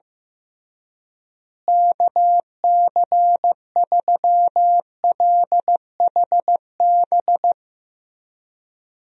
Morse code, heard as a keyed tone, about 15 wpm, 700 Hz. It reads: KC3LHB
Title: USA Call Signs - 6 Characters - Repeated 3x - 15wpm